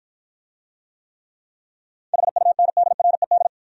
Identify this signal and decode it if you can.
Morse code, heard as a keyed tone, about 53 wpm, 700 Hz. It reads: HUNDRED